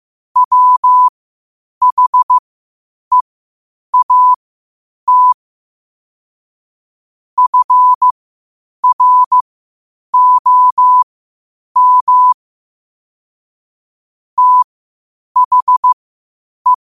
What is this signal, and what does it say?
Morse code, heard as a keyed tone, about 15 wpm, spaced as for 8 wpm, 1000 Hz. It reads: WHEAT FROM THE
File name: qotdfiles / 8wpm